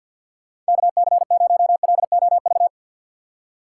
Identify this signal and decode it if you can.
Morse code, heard as a keyed tone, about 53 wpm, 700 Hz. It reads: KC0LOV
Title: USA Call Signs - 6 Characters - 8x Character Spacing - 50wpm